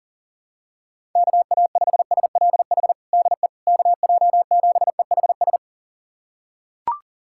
Morse code, heard as a keyed tone, about 40 wpm, 700 Hz. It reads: KA5SLH DE KJ7EHS
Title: USA Callsign DE Callsign 40wpm